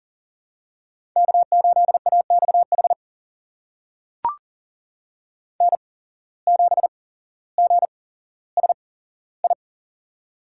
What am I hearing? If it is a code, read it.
Morse code, heard as a keyed tone, about 40 wpm, spaced as for 17 wpm, 700 Hz. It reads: K8AXH N7GSI
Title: USA Call Signs - 5 Characters - 8x Character Spacing - 40wpm